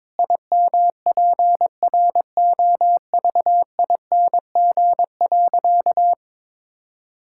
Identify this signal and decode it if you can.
Morse code, heard as a keyed tone, about 22 wpm, 700 Hz. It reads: IMPROVING.
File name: practice / 22wpm / Sentences with 8-Word Phrases - 22wpm